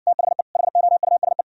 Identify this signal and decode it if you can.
Morse code, heard as a keyed tone, about 60 wpm, 700 Hz. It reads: THE HOUSE